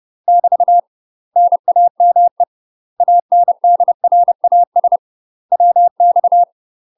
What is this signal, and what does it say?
Morse code, heard as a keyed tone, about 30 wpm, 700 Hz. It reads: <BT> NAME ANDRAS WX
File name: UK MASTER QSOs TRAINING / 30WPM / UK MASTER QSOs 0051